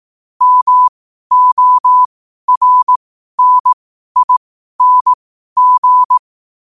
Morse code, heard as a keyed tone, about 18 wpm, spaced as for 12 wpm, 1000 Hz. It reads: MORNING
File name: qotdfiles / 12wpm